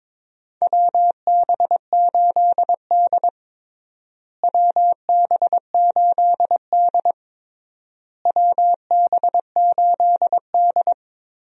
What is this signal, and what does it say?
Morse code, heard as a keyed tone, about 21 wpm, 700 Hz. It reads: WB8D WB8D WB8D
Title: USA Call Signs - 4 Characters - Repeated 3x - 22wpm